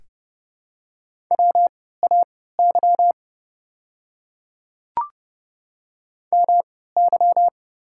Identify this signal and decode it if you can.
Morse code, heard as a keyed tone, about 30 wpm, spaced as for 17 wpm, 700 Hz. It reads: WAY MY
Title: 3x Character Spacing - Top 100 Words - 30wpm